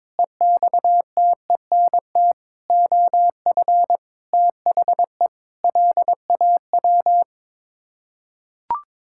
Morse code, heard as a keyed tone, about 22 wpm, 700 Hz. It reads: EXTENT OF THE LAW
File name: practice / 22wpm / Sets of 7 Words - 22wpm